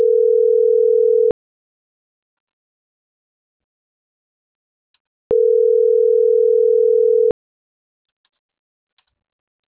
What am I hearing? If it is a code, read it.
Ringback tone (North America), 440 + 480 Hz.